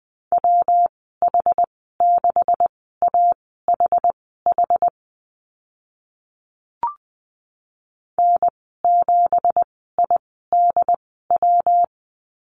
Morse code, heard as a keyed tone, about 20 wpm, spaced as for 14 wpm, 700 Hz. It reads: WH6AHH N7IDW
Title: 219 Lesson 35g - Number 0 - Call Signs 20wpm